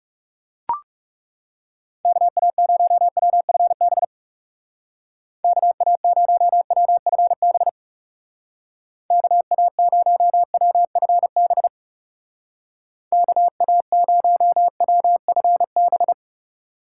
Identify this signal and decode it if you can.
Morse code, heard as a keyed tone, about 36 wpm, 700 Hz. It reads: KA0WFB KA0WFB KA0WFB KA0WFB
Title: Speed Racing - USA 2x3 Call Signs 30wpm